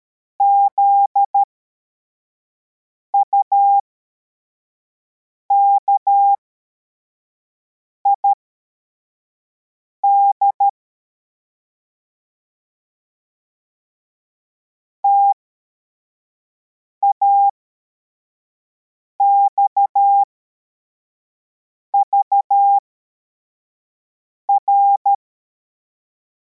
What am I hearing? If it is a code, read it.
Morse code, heard as a keyed tone, about 13 wpm, spaced as for 4 wpm, 800 Hz. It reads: ZUKID TAXVR